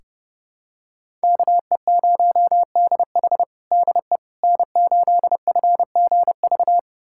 Morse code, heard as a keyed tone, about 32 wpm, 700 Hz. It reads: KE0DH DE N8FGV
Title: USA Callsign DE Callsign - 5 Characters - 30wpm